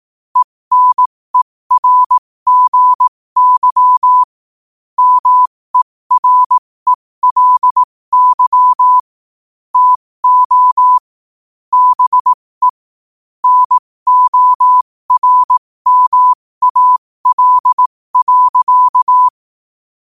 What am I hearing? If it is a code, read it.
Morse code, heard as a keyed tone, about 18 wpm, spaced as for 15 wpm, 1000 Hz. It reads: ENERGY MERELY TO BE NORMAL.
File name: qotdfiles / 15wpm